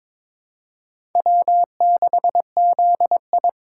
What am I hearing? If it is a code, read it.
Morse code, heard as a keyed tone, about 22 wpm, 700 Hz. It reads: W6ZI
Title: USA Call Signs - 4 Characters - 2x Character Spacing - 22wpm